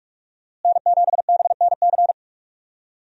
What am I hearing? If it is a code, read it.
Morse code, heard as a keyed tone, about 47 wpm, 700 Hz. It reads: N7BNC
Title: USA Call Signs - 5 Characters - Repeated 2x - 45wpm